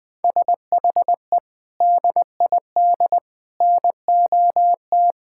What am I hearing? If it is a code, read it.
Morse code, heard as a keyed tone, about 21 wpm, 700 Hz. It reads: SHE DID NOT
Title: Sentences with 3-Word Phrases - 20wpm